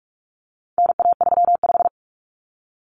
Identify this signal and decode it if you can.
Morse code, heard as a keyed tone, about 44 wpm, 700 Hz. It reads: NA35
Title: Common Cut Numbers - 4-Digit 45wpm